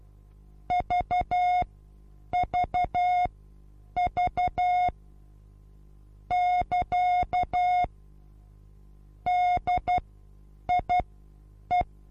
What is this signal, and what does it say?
Morse code, heard as a keyed tone, about 12 wpm, spaced as for 8 wpm, 720 Hz. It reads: VVV <KA> DIE
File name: Onlinelehrgang / morsen / DJ4UF Morsen23